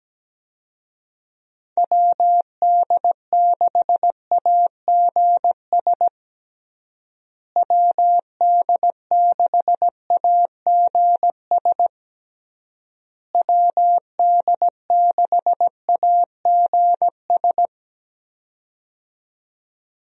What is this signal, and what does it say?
Morse code, heard as a keyed tone, about 17 wpm, 700 Hz. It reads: WD6AGS WD6AGS WD6AGS
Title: USA Call Signs - 6 Characters - Repeated 3x - 17wpm